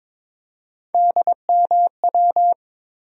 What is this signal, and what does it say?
Morse code, heard as a keyed tone, about 22 wpm, 700 Hz. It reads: DMW
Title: USA Call Sign Postfixes - 3 Letters 22wpm